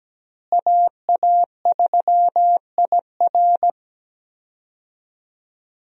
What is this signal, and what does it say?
Morse code, heard as a keyed tone, about 17 wpm, 700 Hz. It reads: AA3IR
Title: USA Call Signs - 5 Characters - Repeated 3x - 17wpm